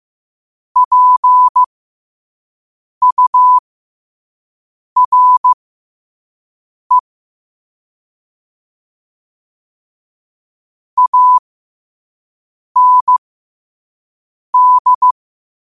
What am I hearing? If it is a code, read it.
Morse code, heard as a keyed tone, about 15 wpm, spaced as for 5 wpm, 1000 Hz. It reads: PURE AND
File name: qotdfiles / 5wpm